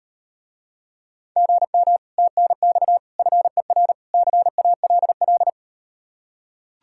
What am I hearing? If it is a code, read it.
Morse code, heard as a keyed tone, about 38 wpm, 700 Hz. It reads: GM TNX FER CALL